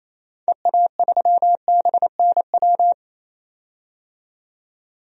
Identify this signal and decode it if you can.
Morse code, heard as a keyed tone, about 28 wpm, 700 Hz. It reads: EA3BNW
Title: DXCC-100 Call Signs - 6 Characters - 28wpm